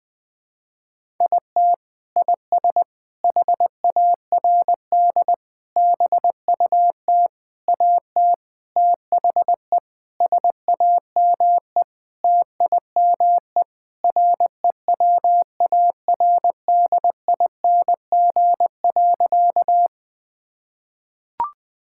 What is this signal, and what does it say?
Morse code, heard as a keyed tone, about 21 wpm, 700 Hz. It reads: IT IS HARD BUT AT THE SAME TIME REWARDING.
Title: Sentences with 5-Word Phrases - 20wpm